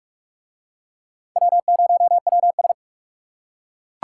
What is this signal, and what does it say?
Morse code, heard as a keyed tone, about 40 wpm, 700 Hz. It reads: W0WS